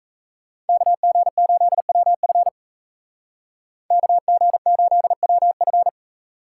Morse code, heard as a keyed tone, about 36 wpm, 700 Hz. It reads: KG8WF KG8WF